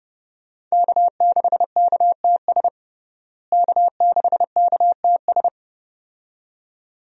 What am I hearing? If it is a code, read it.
Morse code, heard as a keyed tone, about 30 wpm, 700 Hz. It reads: K6KTS K6KTS
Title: USA Call Signs - 5 Characters - Repeated 2x - 30wpm